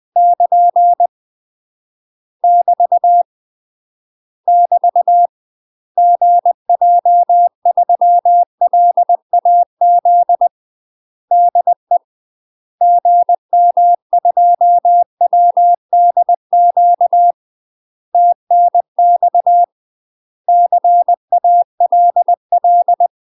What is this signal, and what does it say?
Morse code, heard as a keyed tone, about 20 wpm, 700 Hz. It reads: <KN> <BT> <BT> GJ3LAZ DE GM2WDQ TNX CALL
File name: UK MASTER QSOs TRAINING / 20WPM / UK MASTER QSOs 0028